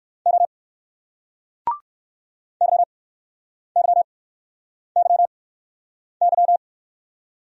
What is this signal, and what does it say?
Morse code, heard as a keyed tone, about 54 wpm, 700 Hz. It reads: O Y Y Y Y
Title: Speed-Racing Single Letter-Number 45wpm